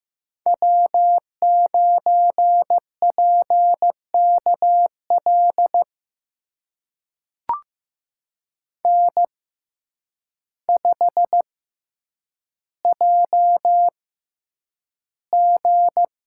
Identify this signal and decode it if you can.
Morse code, heard as a keyed tone, about 15 wpm, spaced as for 8 wpm, 700 Hz. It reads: W9PKL N5JG
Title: USA Call Signs - 6x Character Spacing - 15wpm